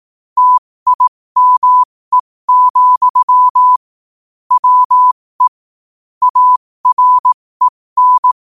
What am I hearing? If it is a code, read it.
Morse code, heard as a keyed tone, about 18 wpm, spaced as for 15 wpm, 1000 Hz. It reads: TIME, WE AREN
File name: qotdfiles / 15wpm